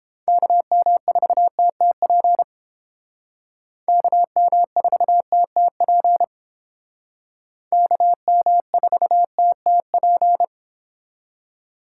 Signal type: Morse code

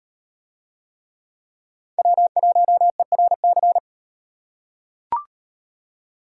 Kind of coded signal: Morse code

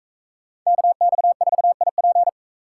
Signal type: Morse code

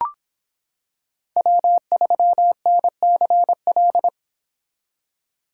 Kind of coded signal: Morse code

